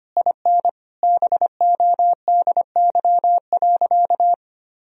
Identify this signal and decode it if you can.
Morse code, heard as a keyed tone, about 25 wpm, 700 Hz. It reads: IN BODY.
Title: Sentences from Top 1000 Words - Review Part 4 - 25wpm